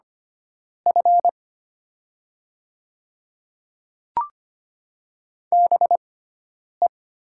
Morse code, heard as a keyed tone, about 25 wpm, spaced as for 9 wpm, 700 Hz. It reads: F BE
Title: 6x Character Spacing - Top 200 Words - 25wpm